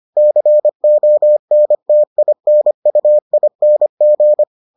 Morse code, heard as a keyed tone, about 25 wpm, 600 Hz. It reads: CONTINUING